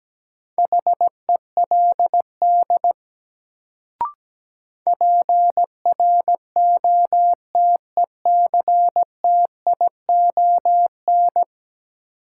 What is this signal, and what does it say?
Morse code, heard as a keyed tone, about 17 wpm, 700 Hz. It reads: HELD PROTECTION
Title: Sentences from Top 1600 Words - New Word Drill Alt - 17wpm